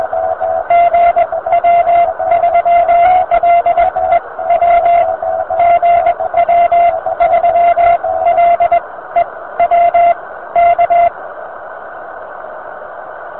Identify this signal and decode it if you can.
Morse code, heard as a keyed tone, about 21 wpm, spaced as for 16 wpm, 690 Hz, with radio noise. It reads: GW3LEW GW3LEW K
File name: VP9-N0JK-pile